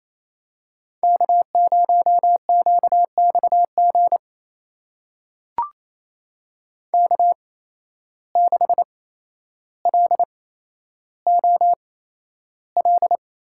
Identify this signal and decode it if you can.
Morse code, heard as a keyed tone, about 28 wpm, spaced as for 12 wpm, 700 Hz. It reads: K0QXG K6LOL
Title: USA Call Signs - 5 Characters - 8x Character Spacing - 28wpm